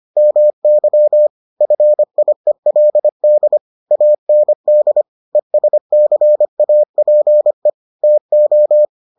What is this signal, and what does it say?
Morse code, heard as a keyed tone, about 25 wpm, 600 Hz. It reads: MY FIELD AND ESCAPE TO